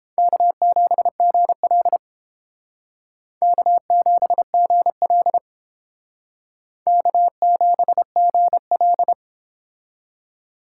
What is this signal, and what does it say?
Morse code, heard as a keyed tone, about 32 wpm, 700 Hz. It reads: K7GL K7GL K7GL